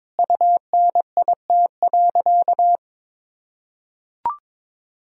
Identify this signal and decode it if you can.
Morse code, heard as a keyed tone, about 22 wpm, 700 Hz. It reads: UNIT.
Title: Sentences from Top 1000 Words - Review Part 2 - 22wpm